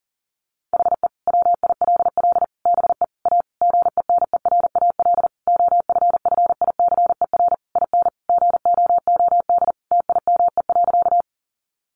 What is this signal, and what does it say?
Morse code, heard as a keyed tone, about 40 wpm, 700 Hz. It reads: HE WILL BE A GENERAL OFFICER IN GOOD TIME.
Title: Sentences from Top 1200 Words 40wpm